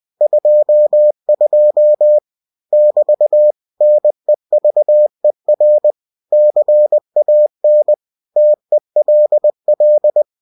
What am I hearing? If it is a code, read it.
Morse code, heard as a keyed tone, about 20 wpm, 600 Hz. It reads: 22 <BT> NEVER CAN TELL
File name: cw-books / winnie-the-pooh / winnie-20wpm-page-0022